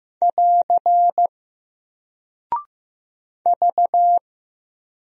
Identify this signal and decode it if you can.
Morse code, heard as a keyed tone, about 15 wpm, 700 Hz. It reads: <AR> V
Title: Single Character (Letter:Number:Punctuation:Prosign) Alt 15wpm